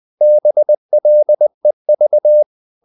Morse code, heard as a keyed tone, about 20 wpm, 600 Hz. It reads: BLEV
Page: Ferie 20 wpm